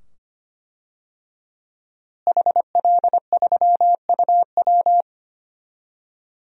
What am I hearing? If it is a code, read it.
Morse code, heard as a keyed tone, about 25 wpm, 700 Hz. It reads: HL3UW
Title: DXCC-100 Call Signs - 5 Characters - Repeated 2x - 25wpm